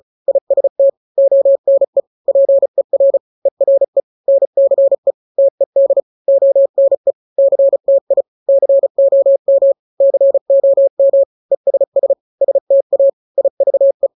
Morse code, heard as a keyed tone, about 35 wpm, 550 Hz. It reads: IST ONE PER ERE NCE TED ONE CTI COM COM ESS STA IVE
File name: 3-grams 35 30 WPM